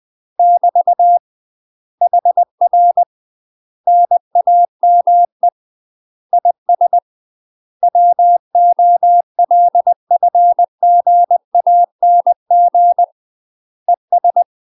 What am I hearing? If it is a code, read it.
Morse code, heard as a keyed tone, about 20 wpm, 700 Hz. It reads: <BT> HR NAME IS WOLFGANG ES
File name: UK MASTER QSOs TRAINING / 20WPM / UK MASTER QSOs 0063